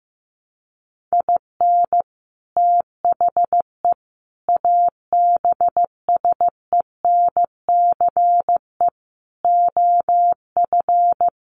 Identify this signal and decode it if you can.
Morse code, heard as a keyed tone, about 15 wpm, 700 Hz. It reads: IN THE ABSENCE OF
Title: Sets of 4 Words 15wpm